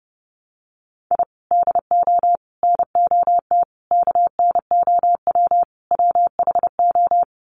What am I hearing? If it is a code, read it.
Morse code, heard as a keyed tone, about 30 wpm, 700 Hz. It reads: I DO NOT KNOW WHO
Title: Sets of 5 Words 30wpm